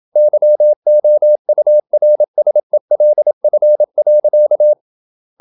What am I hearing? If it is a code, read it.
Morse code, heard as a keyed tone, about 27 wpm, 600 Hz. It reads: YOURSELF.
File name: Never 0027wpm